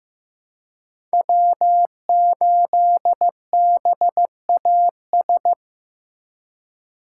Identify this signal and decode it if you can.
Morse code, heard as a keyed tone, about 16 wpm, 700 Hz. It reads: W8BAS
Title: USA Call Signs - 5 Characters - 4x Character Spacing - 15wpm